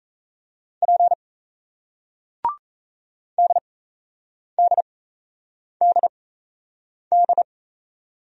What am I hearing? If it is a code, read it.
Morse code, heard as a keyed tone, about 38 wpm, spaced as for 8 wpm, 700 Hz. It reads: P DDDD